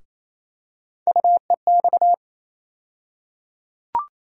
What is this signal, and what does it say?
Morse code, heard as a keyed tone, about 28 wpm, 700 Hz. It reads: UEX
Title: USA Call Sign Postfixes - 3 Letters 28wpm